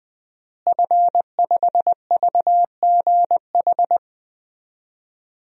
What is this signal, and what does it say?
Morse code, heard as a keyed tone, about 20 wpm, 700 Hz. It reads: F5VGH
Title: DXCC-100 Call Signs - 5 Characters - Repeated 3x - 20wpm